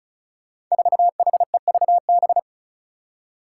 Morse code, heard as a keyed tone, about 36 wpm, 700 Hz. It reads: 4HEVB
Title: Five Challenging Letter-Number Combinations 35wpm